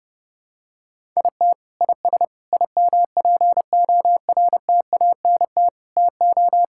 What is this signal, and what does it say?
Morse code, heard as a keyed tone, about 32 wpm, 700 Hz. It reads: IT IS IMPORTANT TO
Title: Sentences from Top 1000 Words - Review Part 6 - 30wpm